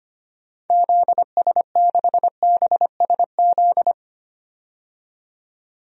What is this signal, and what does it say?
Morse code, heard as a keyed tone, about 25 wpm, 700 Hz. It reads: ZS6BSZ
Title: DXCC-100 Call Signs - 6 Characters - 25wpm